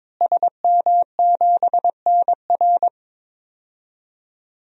Morse code, heard as a keyed tone, about 22 wpm, 700 Hz. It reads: SM7NR
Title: DXCC-100 Call Signs - 5 Characters - Repeated 3x - 22wpm